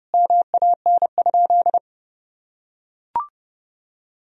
Morse code, heard as a keyed tone, about 32 wpm, 700 Hz. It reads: MAN?
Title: Sentences from Top 1000 Words - Review Part 3 - 30wpm